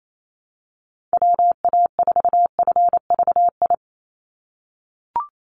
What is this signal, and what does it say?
Morse code, heard as a keyed tone, about 28 wpm, 700 Hz. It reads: WA4FVI
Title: USA Call Signs 28wpm